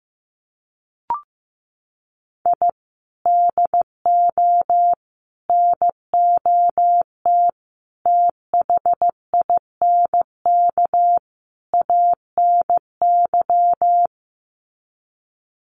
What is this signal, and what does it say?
Morse code, heard as a keyed tone, about 15 wpm, 700 Hz. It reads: I DO NOT THINK ANY
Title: Sets of 5 Words Encore 15wpm